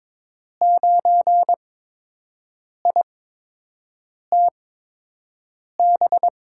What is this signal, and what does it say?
Morse code, heard as a keyed tone, about 22 wpm, spaced as for 6 wpm, 700 Hz. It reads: 9ITB
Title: USA Call Signs - 5 Characters - 8x Character Spacing - 22wpm